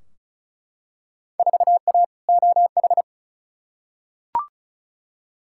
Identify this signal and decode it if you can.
Morse code, heard as a keyed tone, about 36 wpm, 700 Hz. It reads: 4A OH
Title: Field Day Exchanges - 35wpm